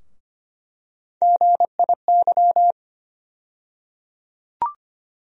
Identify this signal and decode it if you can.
Morse code, heard as a keyed tone, about 25 wpm, 700 Hz. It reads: GIY
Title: USA Call Sign Postfixes - 3 Letters 25wpm